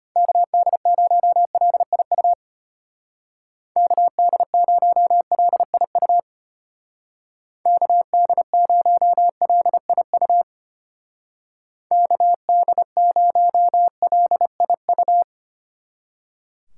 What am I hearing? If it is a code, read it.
Morse code, heard as a keyed tone, about 32 wpm, 700 Hz. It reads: KD0LIU KD0LIU KD0LIU KD0LIU